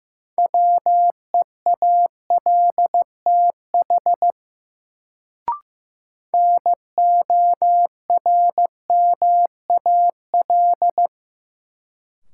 Morse code, heard as a keyed tone, about 15 wpm, 700 Hz. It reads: WEALTH NORMAL
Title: Sentences from Top 1200 Words - New Word Drill Alt - 15wpm